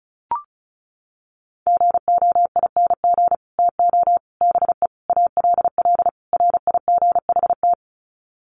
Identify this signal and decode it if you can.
Morse code, heard as a keyed tone, about 34 wpm, 700 Hz. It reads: GOING TO BE ALL RIGHT
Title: Sets of 5 Words Encore2 35wpm